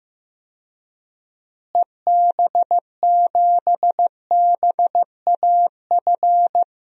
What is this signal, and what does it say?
Morse code, heard as a keyed tone, about 16 wpm, 700 Hz. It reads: EB7BAF
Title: DXCC-100 Call Signs - 6 Characters - Repeated 3x - 15wpm